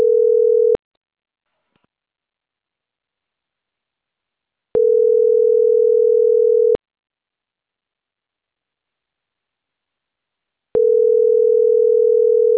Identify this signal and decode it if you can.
Ringback tone (North America), 440 + 480 Hz.